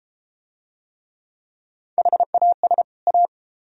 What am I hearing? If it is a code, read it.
Morse code, heard as a keyed tone, about 31 wpm, 700 Hz. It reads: HAS A